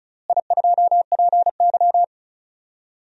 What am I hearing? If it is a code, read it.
Morse code, heard as a keyed tone, about 36 wpm, 700 Hz. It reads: I2PY
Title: DXCC-100 Call Signs - 4 Characters - Repeated 3x - 35wpm